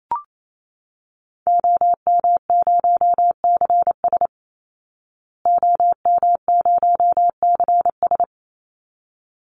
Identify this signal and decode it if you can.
Morse code, heard as a keyed tone, about 28 wpm, 700 Hz. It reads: OM0CS OM0CS
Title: Call Signs Repeated 3x - 28wpm